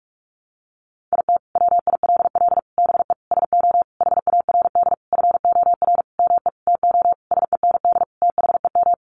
Morse code, heard as a keyed tone, about 44 wpm, 700 Hz. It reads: IT WILL BE SO HARD FOR ME TO SEND THEM
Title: Sentences from Top 1500 Words 45wpm